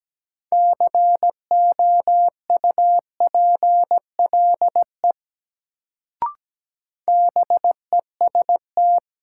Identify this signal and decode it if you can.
Morse code, heard as a keyed tone, about 17 wpm, 700 Hz. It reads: COUPLE BEST
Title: Sets of 2 Words - New Word Drill - 17wpm